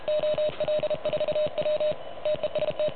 Morse code, heard as a keyed tone, about 31 wpm, 590 Hz, with noise. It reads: OL4W TEST
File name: truSDX / MWC29.8